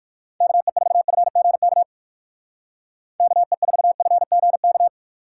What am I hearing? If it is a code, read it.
Morse code, heard as a keyed tone, about 50 wpm, 700 Hz. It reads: KE4FGK KE4FGK
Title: Speed Racing - USA 2x3 Call Signs 45wpm